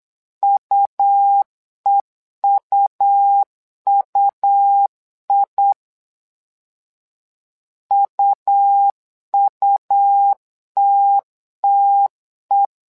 Morse code, heard as a keyed tone, about 8 wpm, 800 Hz. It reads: UEUUI UUTTE
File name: LEZIONI / 2 UHDN / 2 UH 8wpm 2129